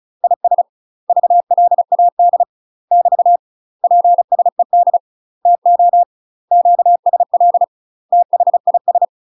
Morse code, heard as a keyed tone, about 34 wpm, 700 Hz. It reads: IS VLAD <BT> PSED TO QSL THIS
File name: UK MASTER QSOs TRAINING / 35WPM / UK MASTER QSOs 0019